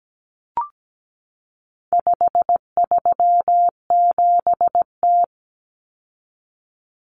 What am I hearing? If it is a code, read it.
Morse code, heard as a keyed tone, about 17 wpm, 700 Hz. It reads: 537T
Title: Common Cut Numbers - 4-Digit 17wpm